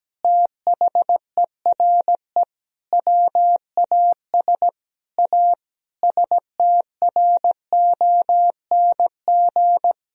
Morse code, heard as a keyed tone, about 17 wpm, 700 Hz. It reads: THERE WAS A STRONG
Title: Sentences from Top 1000 Words - Review Part 4 - 17wpm